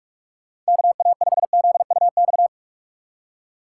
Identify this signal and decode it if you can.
Morse code, heard as a keyed tone, about 59 wpm, 700 Hz. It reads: KA5ZUX